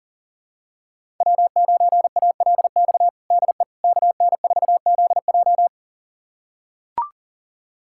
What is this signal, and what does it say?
Morse code, heard as a keyed tone, about 37 wpm, 700 Hz. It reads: W9ALX DE KN4ZJ
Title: USA Callsign DE Callsign - 5 Characters - 40wpm